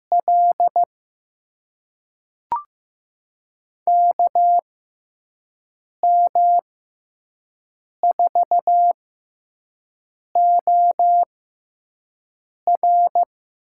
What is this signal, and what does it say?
Morse code, heard as a keyed tone, about 15 wpm, spaced as for 5 wpm, 700 Hz. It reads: L KM4OR
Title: USA Call Signs - 6 Characters - 6x Character Spacing - 15wpm